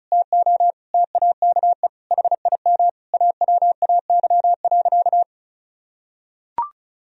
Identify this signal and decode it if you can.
Morse code, heard as a keyed tone, about 34 wpm, 700 Hz. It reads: TO TAKE HIM AWAY.
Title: Sentences from Top 1000 Words - Review Part 4 - 35wpm